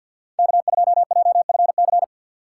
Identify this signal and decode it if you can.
Morse code, heard as a keyed tone, about 47 wpm, 700 Hz. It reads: K2JFC
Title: USA Call Signs - 4x Character Spacing - 50wpm